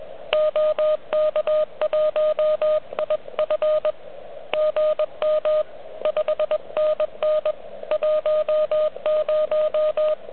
Morse code, heard as a keyed tone, about 21 wpm, 590 Hz. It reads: OK1IF GM 5NN 10
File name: OK QRP 2022 / Zvuk / REC34